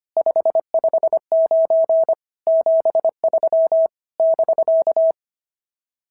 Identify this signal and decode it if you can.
Morse code, heard as a keyed tone, about 24 wpm, 650 Hz. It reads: 559 73 <BK>
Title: SOTA QSO Practice 25wpm